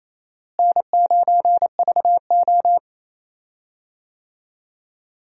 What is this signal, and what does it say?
Morse code, heard as a keyed tone, about 28 wpm, 700 Hz. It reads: N9VO
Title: USA Call Signs - 4 Characters - Repeated 2x - 28wpm